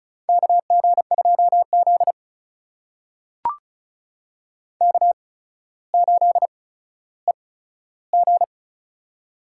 Morse code, heard as a keyed tone, about 34 wpm, spaced as for 15 wpm, 700 Hz. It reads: KG2Z K8EG
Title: USA Call Signs - 4 Characters - 8x Character Spacing - 35wpm